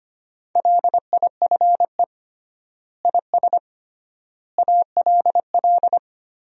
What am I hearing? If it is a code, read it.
Morse code, heard as a keyed tone, about 25 wpm, 700 Hz. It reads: LIFE IS ALL
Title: 3x Word-Spacing - Sentences from Top 300 Words - 25wpm